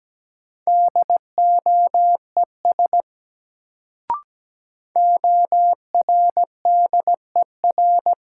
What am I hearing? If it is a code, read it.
Morse code, heard as a keyed tone, about 17 wpm, 700 Hz. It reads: DOES ORDER
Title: Sets of 2 Words - New Word Drill - 17wpm